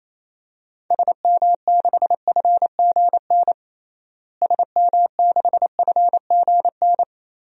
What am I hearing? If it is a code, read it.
Morse code, heard as a keyed tone, about 28 wpm, 700 Hz. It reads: SM6FGN SM6FGN
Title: DXCC-100 Call Signs - 6 Characters - Repeated 3x - 28wpm